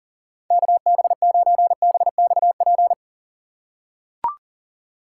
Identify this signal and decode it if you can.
Morse code, heard as a keyed tone, about 40 wpm, 700 Hz. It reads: KB9BXP
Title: USA Call Signs - 3x Character Spacing - 40wpm